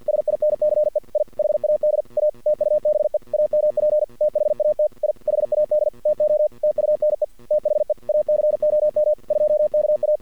Morse code, heard as a keyed tone, about 50 wpm, 610 Hz. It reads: SINCE I HAD A TUBE AMP THAT I HAD TO TUNE THE ACOM 2KA
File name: archived / CRSnet